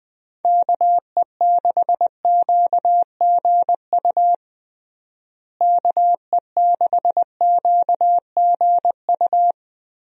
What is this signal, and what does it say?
Morse code, heard as a keyed tone, about 20 wpm, 700 Hz. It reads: KE6QGU KE6QGU
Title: USA Call Signs - 6 Characters - Repeated 3x - 20wpm